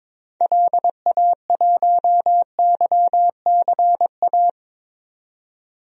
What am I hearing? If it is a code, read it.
Morse code, heard as a keyed tone, about 21 wpm, 700 Hz. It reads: LA1YCA
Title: DXCC-100 Call Signs - 6 Characters - 22wpm